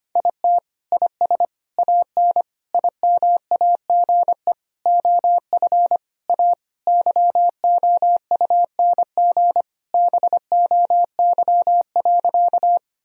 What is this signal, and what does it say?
Morse code, heard as a keyed tone, about 25 wpm, 700 Hz. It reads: IT IS AN IMAGE OF A YOUNG BOY.
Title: Sentences from Top 1000 Words - Review Part 4 - 25wpm